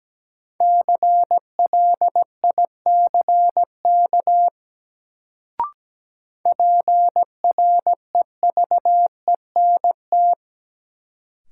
Morse code, heard as a keyed tone, about 17 wpm, 700 Hz. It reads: CLICK PREVENT
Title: Sentences from Top 1100 Words - New Word Drill - 17wpm